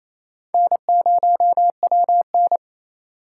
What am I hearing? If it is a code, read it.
Morse code, heard as a keyed tone, about 28 wpm, 700 Hz. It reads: N0WN
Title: USA Call Signs - 4 Characters - 3x Character Spacing - 28wpm